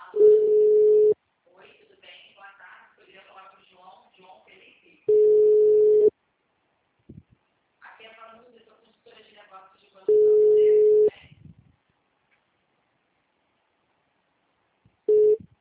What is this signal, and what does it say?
Ringback tone (Europe, 425 Hz), with speech.